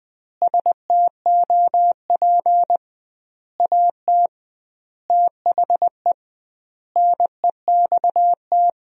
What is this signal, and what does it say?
Morse code, heard as a keyed tone, about 20 wpm, 700 Hz. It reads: STOP AT THE NEXT
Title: 2x Word-Spacing - Sentences from Top 400 Words - 20wpm